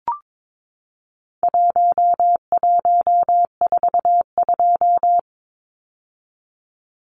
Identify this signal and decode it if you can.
Morse code, heard as a keyed tone, about 22 wpm, 700 Hz. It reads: 1142
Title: Numbers - Contest Style - 4-Digit 22wpm